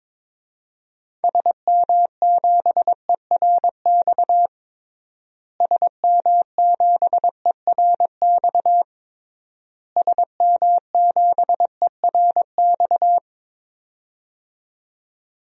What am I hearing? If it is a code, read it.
Morse code, heard as a keyed tone, about 22 wpm, 700 Hz. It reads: SM7ERX SM7ERX SM7ERX
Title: DXCC-100 Call Signs - 6 Characters - Repeated 3x - 22wpm